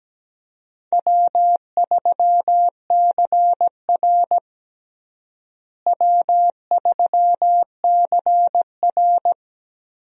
Morse code, heard as a keyed tone, about 17 wpm, 700 Hz. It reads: W3CR W3CR
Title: USA Call Signs - 4 Characters - Repeated 2x - 17wpm